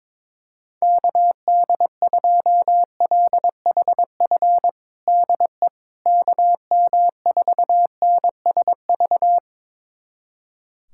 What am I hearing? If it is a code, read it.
Morse code, heard as a keyed tone, about 21 wpm, 700 Hz. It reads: KD2LHF DE KM4NSV
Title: USA Callsign DE Callsign - 6 Characters - 22wpm